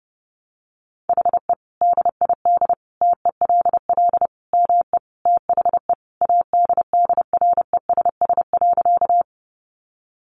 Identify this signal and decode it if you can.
Morse code, heard as a keyed tone, about 30 wpm, 700 Hz. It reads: HE DID TELL ME THE ADDRESS.
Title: Sentences from Top 1000 Words 30wpm